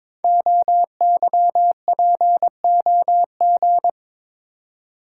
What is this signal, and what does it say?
Morse code, heard as a keyed tone, about 22 wpm, 700 Hz. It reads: OYPOG
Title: Code Groups - 5 Letters - 22wpm